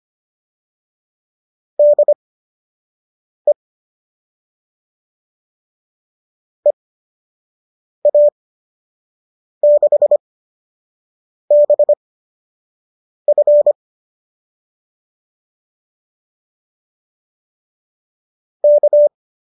Morse code, heard as a keyed tone, about 25 wpm, spaced as for 6 wpm, 570 Hz. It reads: DE EA6BF K